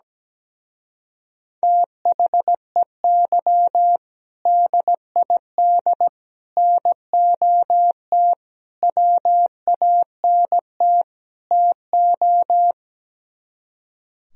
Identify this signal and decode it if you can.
Morse code, heard as a keyed tone, about 17 wpm, 700 Hz. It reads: THEY DID NOT WANT TO
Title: Speed-Racing - Sets of 5 Words 17wpm